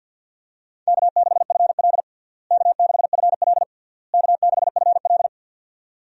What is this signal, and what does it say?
Morse code, heard as a keyed tone, about 53 wpm, 700 Hz. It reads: K6FL K6FL K6FL
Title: USA Call Signs - 4 Characters - Repeated 3x - 50wpm